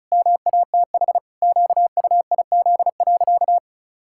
Morse code, heard as a keyed tone, about 36 wpm, 700 Hz. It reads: MATH QUIZ.
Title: Pangram Challenge v2 - 35wpm